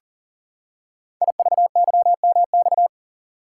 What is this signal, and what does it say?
Morse code, heard as a keyed tone, about 38 wpm, 690 Hz. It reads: IVYMX
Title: Code Groups - 5 Letters - 40wpm